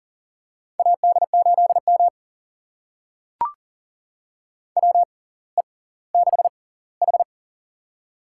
Morse code, heard as a keyed tone, about 39 wpm, spaced as for 20 wpm, 700 Hz. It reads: AD8M WE6H
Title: USA Call Signs - 4 Characters - 6x Character Spacing - 40wpm